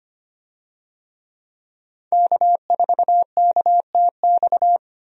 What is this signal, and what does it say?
Morse code, heard as a keyed tone, about 25 wpm, 700 Hz. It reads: K4KTX
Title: USA Call Signs - 5 Characters - 25wpm